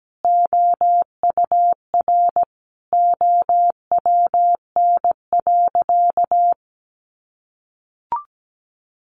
Morse code, heard as a keyed tone, about 17 wpm, 700 Hz. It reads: OUR OWN.
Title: Sentences from Top 1500 Words 17wpm